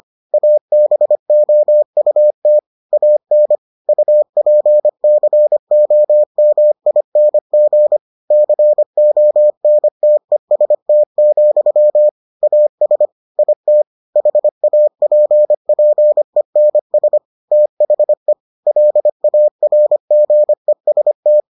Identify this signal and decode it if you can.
Morse code, heard as a keyed tone, about 25 wpm, 600 Hz. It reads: ABOUT AN UPCOMING CONTEST, AS IT HAPPENS THE LARGEST